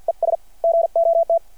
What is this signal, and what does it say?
Morse code, heard as a keyed tone, about 52 wpm, 660 Hz. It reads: ES GOT
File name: archived / CRSnet